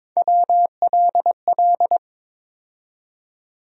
Morse code, heard as a keyed tone, about 22 wpm, 700 Hz. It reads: WLL
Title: USA Call Sign Postfixes - 3 Letters 22wpm